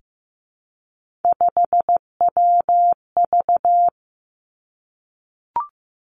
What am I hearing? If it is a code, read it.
Morse code, heard as a keyed tone, about 15 wpm, 700 Hz. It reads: 5WV
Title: Three Challenging Letter-Number Combinations 15wpm